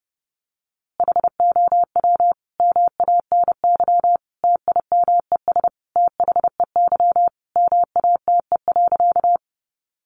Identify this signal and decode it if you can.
Morse code, heard as a keyed tone, about 30 wpm, 700 Hz. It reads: HOW MANY TIMES THEY MATE.
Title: A Sound Of Thunder 30wpm